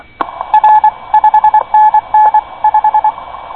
Morse code, heard as a keyed tone, about 24 wpm, 860 Hz. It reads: R 5NN 5
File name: co8tw / cqwwcw2010